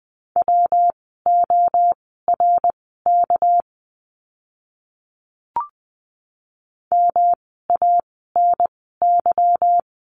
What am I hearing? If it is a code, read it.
Morse code, heard as a keyed tone, about 20 wpm, spaced as for 14 wpm, 700 Hz. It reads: WORK MANY
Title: 237 Lesson 38f - Letter q - Words 20wpm